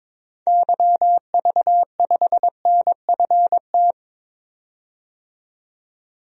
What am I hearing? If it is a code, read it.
Morse code, heard as a keyed tone, about 22 wpm, 700 Hz. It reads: YV5NFT
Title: DXCC-100 Call Signs - 6 Characters - 22wpm